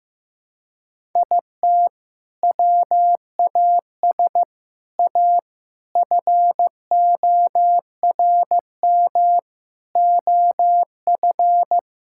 Morse code, heard as a keyed tone, about 16 wpm, 700 Hz. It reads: IT WAS A FORM OF
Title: Sentences from Top 1000 Words - Review Part 5 - 15wpm